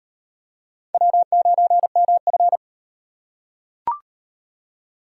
Morse code, heard as a keyed tone, about 33 wpm, 700 Hz. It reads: W9MF